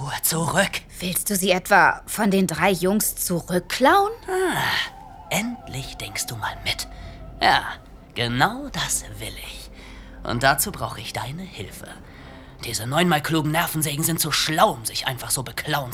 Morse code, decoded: none